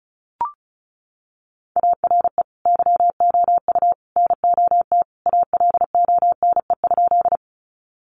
Morse code, heard as a keyed tone, about 34 wpm, 700 Hz. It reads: ARE YOU NOT ALONE?
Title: Sentences from Top 1000 Words 35wpm